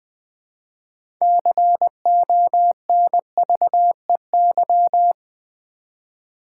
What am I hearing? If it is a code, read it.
Morse code, heard as a keyed tone, about 20 wpm, 700 Hz. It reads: CONVEY
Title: 6 Letter Words - First 3 Letters Given - 20wpm